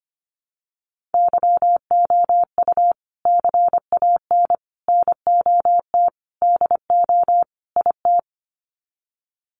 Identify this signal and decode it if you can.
Morse code, heard as a keyed tone, about 25 wpm, 700 Hz. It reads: YOU CAN NOT DO IT
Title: Sets of 5 Words Encore2 25wpm